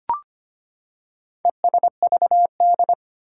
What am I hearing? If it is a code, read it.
Morse code, heard as a keyed tone, about 25 wpm, 700 Hz. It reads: ESVD
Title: Four Challenging Letter-Number Combinations 25wpm